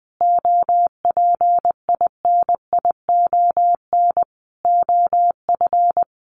Morse code, heard as a keyed tone, about 20 wpm, 700 Hz. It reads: OPINION OF
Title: Sentences from Top 1400 Words 20wpm